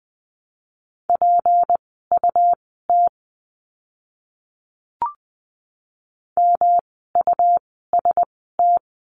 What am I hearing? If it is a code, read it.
Morse code, heard as a keyed tone, about 20 wpm, spaced as for 14 wpm, 700 Hz. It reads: PUT MUST
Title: 237 Lesson 38f - Letter q - Words 20wpm